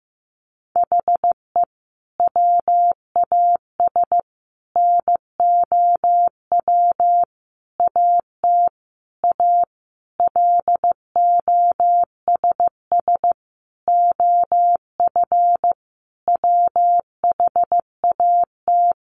Morse code, heard as a keyed tone, about 15 wpm, 700 Hz. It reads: HE WAS NOW AT A LOSS OF WHAT
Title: Sentences from Top 1400 Words 15wpm